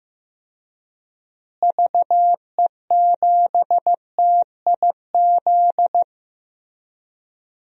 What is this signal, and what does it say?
Morse code, heard as a keyed tone, about 15 wpm, 700 Hz. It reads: VE7TIZ
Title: DXCC-100 Call Signs - 6 Characters - Repeated 2x - 15wpm